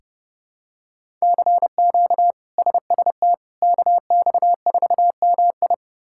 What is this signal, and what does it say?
Morse code, heard as a keyed tone, about 30 wpm, 700 Hz. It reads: CQ SST KX4MI
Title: K1USN SST QSO Practice - 30wpm